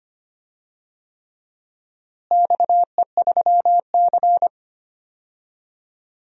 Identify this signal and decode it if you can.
Morse code, heard as a keyed tone, about 25 wpm, 700 Hz. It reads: XE3C
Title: DXCC-100 Call Signs - 4 Characters - Repeated 3x - 25wpm